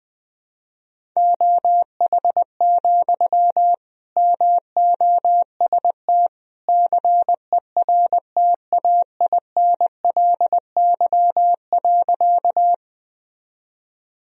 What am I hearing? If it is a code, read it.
Morse code, heard as a keyed tone, about 20 wpm, 700 Hz. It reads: OH, MOST CERTAINLY.
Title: Sentences from Top 1000 Words - Review Part 1 - 20wpm